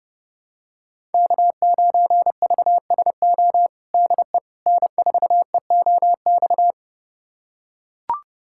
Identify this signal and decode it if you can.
Morse code, heard as a keyed tone, about 30 wpm, 700 Hz. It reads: K9VSO DE N4EOX
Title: USA Callsign DE Callsign - 5 Characters - 30wpm